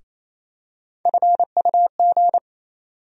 Morse code, heard as a keyed tone, about 28 wpm, 700 Hz. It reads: FUG
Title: USA Call Sign Postfixes - 3 Letters 28wpm